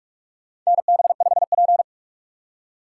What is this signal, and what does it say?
Morse code, heard as a keyed tone, about 44 wpm, 700 Hz. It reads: NB5P
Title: USA Call Signs - 4 Characters - 6x Character Spacing - 45wpm